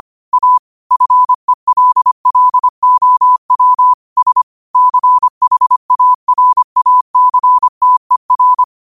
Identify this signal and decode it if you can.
Morse code, heard as a keyed tone, about 25 wpm, 1000 Hz. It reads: A FELLOWS CHARACTER
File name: qotdfiles / 25wpm